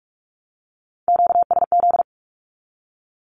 Morse code, heard as a keyed tone, about 44 wpm, 700 Hz. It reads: QSZ
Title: Q-Codes 45wpm